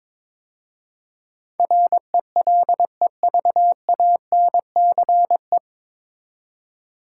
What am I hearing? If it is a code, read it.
Morse code, heard as a keyed tone, about 22 wpm, 700 Hz. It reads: RELEVANCE
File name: 9 Letter Words - First 3 Letters Given - 22wpm